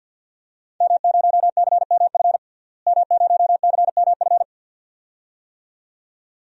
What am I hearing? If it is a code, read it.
Morse code, heard as a keyed tone, about 53 wpm, 700 Hz. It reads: M0CMF M0CMF
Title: DXCC-100 Call Signs - 5 Characters - Repeated 2x - 50wpm